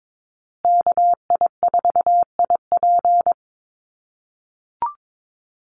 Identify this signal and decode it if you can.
Morse code, heard as a keyed tone, about 22 wpm, 700 Hz. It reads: KI4IP
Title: USA Call Signs Repeated 3x 22wpm